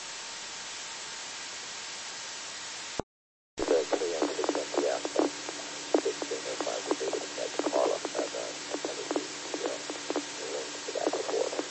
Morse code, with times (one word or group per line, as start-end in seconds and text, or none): none